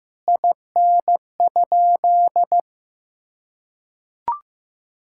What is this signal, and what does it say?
Morse code, heard as a keyed tone, about 15 wpm, 700 Hz. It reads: IN?
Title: Sentences from Top 1600 Words 15wpm